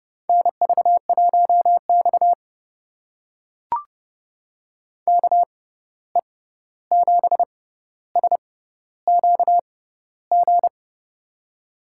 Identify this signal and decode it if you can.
Morse code, heard as a keyed tone, about 30 wpm, spaced as for 14 wpm, 700 Hz. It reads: NV1X KE7SQG
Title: USA Call Signs - 6x Character Spacing - 30wpm